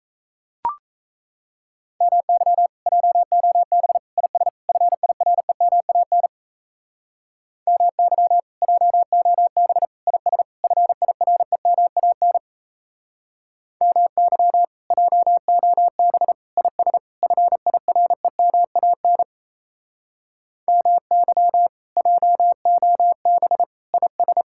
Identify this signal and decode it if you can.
Morse code, heard as a keyed tone, about 36 wpm, 700 Hz. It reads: MY JOB IS FIREMAN MY JOB IS FIREMAN MY JOB IS FIREMAN MY JOB IS